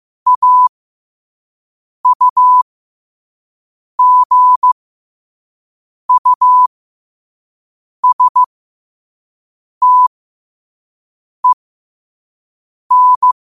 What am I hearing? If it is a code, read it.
Morse code, heard as a keyed tone, about 15 wpm, spaced as for 5 wpm, 1000 Hz. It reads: AUGUSTEN
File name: qotdfiles / 5wpm